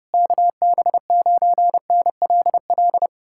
Morse code, heard as a keyed tone, about 33 wpm, 700 Hz. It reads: KB9NLL